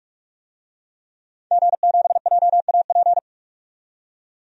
Morse code, heard as a keyed tone, about 46 wpm, 700 Hz. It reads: G7JAP
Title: DXCC-100 Call Signs - 5 Characters - 45wpm